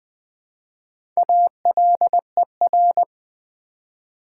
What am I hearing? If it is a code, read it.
Morse code, heard as a keyed tone, about 20 wpm, 700 Hz. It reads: ALER
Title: Build A Word - 1 Letter At A Time - 3 to 5-Letter Words - 20wpm